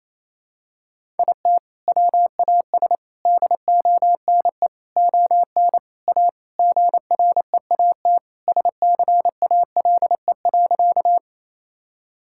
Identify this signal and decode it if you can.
Morse code, heard as a keyed tone, about 28 wpm, 700 Hz. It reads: IT WAS DONE ON A GREAT SCALE.
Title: Sentences from Top 1000 Words - Review Part 5 - 28wpm